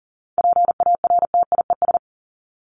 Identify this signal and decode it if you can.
Morse code, heard as a keyed tone, about 40 wpm, 700 Hz. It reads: PARTIES
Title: ES:S Suffix - Top 100 Words with Base Words 40wpm